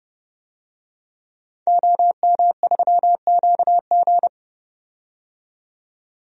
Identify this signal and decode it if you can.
Morse code, heard as a keyed tone, about 31 wpm, 700 Hz. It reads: OM3QG
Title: DXCC-100 Call Signs - 5 Characters - 30wpm